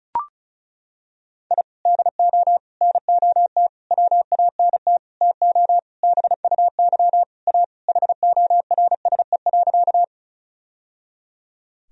Morse code, heard as a keyed tone, about 34 wpm, 700 Hz. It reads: I DO NOT WANT TO BUY A HORSE.
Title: Sentences from Top 1000 Words - Review Part 4 - 35wpm